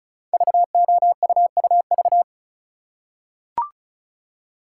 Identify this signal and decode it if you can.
Morse code, heard as a keyed tone, about 36 wpm, 690 Hz. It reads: VOUUV
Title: Code Groups - 5 Letters - 35wpm